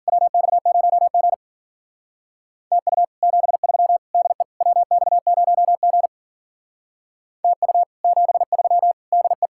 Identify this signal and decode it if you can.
Morse code, heard as a keyed tone, about 51 wpm, 700 Hz. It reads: WX0G TU 73 DE WX0G TU 73 DE